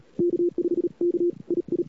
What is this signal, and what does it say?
Morse code, heard as a keyed tone, about 39 wpm, 380 Hz. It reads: K5 K II
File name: CWSprint / Feb2007 / 2007-feb-cw-sprint-k5tr-op-n5rz -00:00